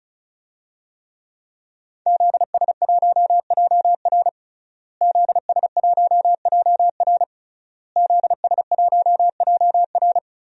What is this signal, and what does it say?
Morse code, heard as a keyed tone, about 36 wpm, 700 Hz. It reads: ZS1JR ZS1JR ZS1JR
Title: DXCC-100 Call Signs - 5 Characters - Repeated 3x - 35wpm